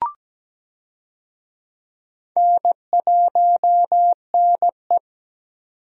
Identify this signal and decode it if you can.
Morse code, heard as a keyed tone, about 17 wpm, 700 Hz. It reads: N1NE
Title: USA Call Signs - 4 Characters - 17wpm